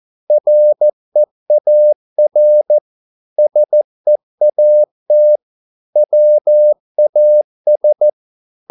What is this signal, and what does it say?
Morse code, heard as a keyed tone, about 14 wpm, 600 Hz. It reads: REAR SEAT WAS